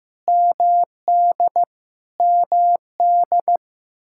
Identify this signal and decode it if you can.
Morse code, heard as a keyed tone, about 15 wpm, 700 Hz. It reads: MD MD
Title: Practice QSO - Exchange 1 - Beginning - 15wpm